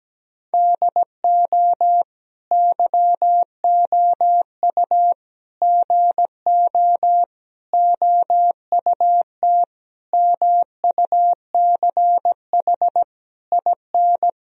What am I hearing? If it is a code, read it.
Morse code, heard as a keyed tone, about 17 wpm, 700 Hz. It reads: DO YOU GO OUT MUCH IN
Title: Sentences from Top 1000 Words - Review Part 4 - 17wpm